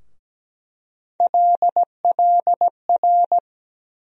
Morse code, heard as a keyed tone, about 17 wpm, 700 Hz. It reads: LLR
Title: USA Call Sign Postfixes - 3 Letters 17wpm